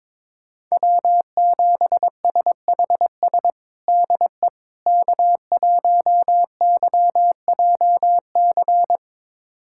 Morse code, heard as a keyed tone, about 22 wpm, 700 Hz. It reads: W7SHS DE K1YJC
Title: USA Callsign DE Callsign - 5 Characters - 22wpm